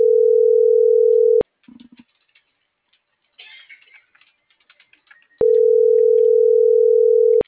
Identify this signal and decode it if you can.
Ringback tone (North America), 440 + 480 Hz.
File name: CENTRAL NMP / out melanie.diaznmp - .38108